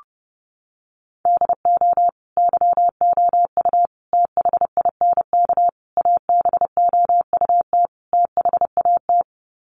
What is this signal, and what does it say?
Morse code, heard as a keyed tone, about 30 wpm, 700 Hz. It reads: DO YOU THINK ABOUT THAT
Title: Sets of 5 Words Encore2 30wpm